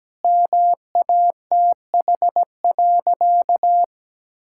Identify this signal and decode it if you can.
Morse code, heard as a keyed tone, about 17 wpm, 700 Hz. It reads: MATH.
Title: Sentences from Top 1000 Words - Review Part 6 - 17wpm